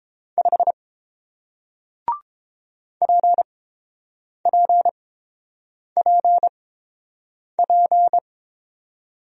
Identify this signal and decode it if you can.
Morse code, heard as a keyed tone, about 30 wpm, spaced as for 7 wpm, 700 Hz. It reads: 5 PPPP